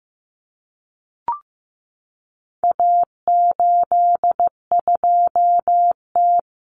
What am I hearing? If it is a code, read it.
Morse code, heard as a keyed tone, about 15 wpm, 700 Hz. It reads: A82T
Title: Common Cut Numbers - 4-Digit 15wpm